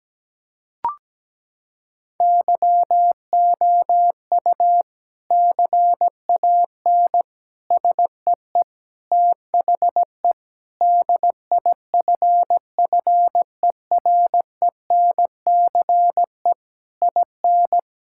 Morse code, heard as a keyed tone, about 17 wpm, 700 Hz. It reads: YOU CAN SEE THE DIFFERENCE IN
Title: Sentences from Top 1000 Words - Review Part 5 - 17wpm